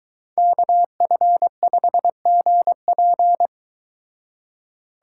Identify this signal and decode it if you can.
Morse code, heard as a keyed tone, about 22 wpm, 700 Hz. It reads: KF5GP